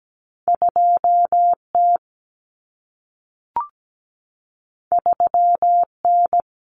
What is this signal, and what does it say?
Morse code, heard as a keyed tone, about 17 wpm, 700 Hz. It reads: 2T 3N
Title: Common Cut Numbers - 2-Digit 17wpm